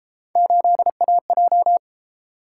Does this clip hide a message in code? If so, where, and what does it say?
Yes: Morse code from 0.4 to 1.8 s, heard as a keyed tone, about 29 wpm, 700 Hz. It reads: 8AJ